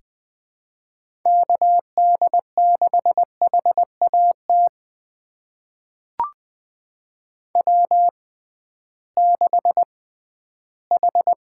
Morse code, heard as a keyed tone, about 20 wpm, spaced as for 13 wpm, 700 Hz. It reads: KD6HAT W6H
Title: USA Call Signs - 6x Character Spacing - 20wpm